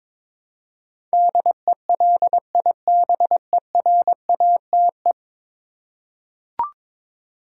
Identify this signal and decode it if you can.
Morse code, heard as a keyed tone, about 22 wpm, 700 Hz. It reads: DELIBERATE
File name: practice / 22wpm / 10 Letter Words - First 3 Letters Given - 22wpm